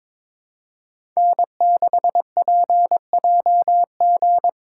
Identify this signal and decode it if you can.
Morse code, heard as a keyed tone, about 22 wpm, 700 Hz. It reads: N6PJG
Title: USA Call Signs - 5 Characters - 3x Character Spacing - 22wpm